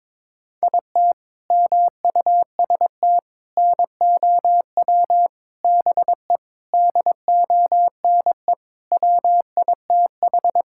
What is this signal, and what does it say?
Morse code, heard as a keyed tone, about 22 wpm, 700 Hz. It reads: IT MUST NOW BE DONE WITH
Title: Sentences from Top 1000 Words - Review Part 5 - 22wpm